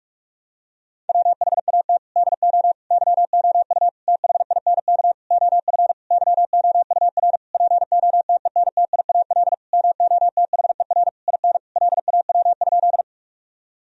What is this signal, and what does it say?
Morse code, heard as a keyed tone, about 46 wpm, 700 Hz. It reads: WHAT DO YOU THINK OF YOUR POTENTIAL MOTHER IN LAW?
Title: Sentences from Top 1000 Words - Review Part 4 - 45wpm